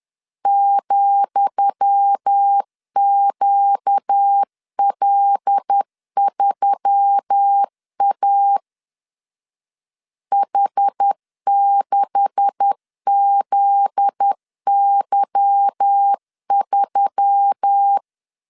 Morse code, heard as a keyed tone, about 10 wpm, 800 Hz. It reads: ,QL3A H6ZY3
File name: LEZIONI / 10 SIGNS / 10D COMA 10wpm 3013